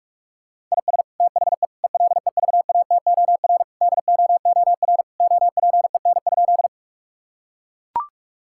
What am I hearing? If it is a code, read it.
Morse code, heard as a keyed tone, about 41 wpm, 700 Hz. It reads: IS THE ELEVATOR DOOR OPEN?
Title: Sentences from Top 1600 Words 45wpm